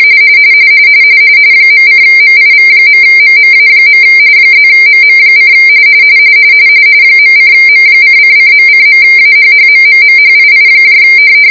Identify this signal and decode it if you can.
RTTY, about 45.45 baud. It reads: .:MIV.MMA. +MM: .:M:::.:.AHHMMMHII